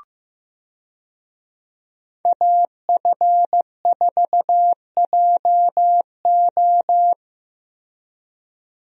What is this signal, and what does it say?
Morse code, heard as a keyed tone, about 15 wpm, 700 Hz. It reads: AF4JO
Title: USA Call Signs - 5 Characters - Repeated 3x - 15wpm